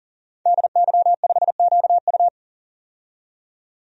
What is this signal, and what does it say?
Morse code, heard as a keyed tone, about 37 wpm, 700 Hz. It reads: DY5QU
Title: DXCC-100 Call Signs - 5 Characters - 40wpm